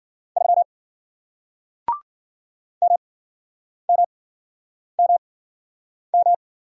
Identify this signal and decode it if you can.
Morse code, heard as a keyed tone, about 53 wpm, spaced as for 9 wpm, 700 Hz. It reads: 3 MMMM